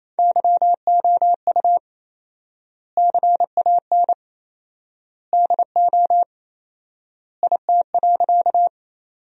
Morse code, heard as a keyed tone, about 28 wpm, 700 Hz. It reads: YOU CAN DO IT.
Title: 4x Word-Spacing - Sentences from Top 100 Words - 28wpm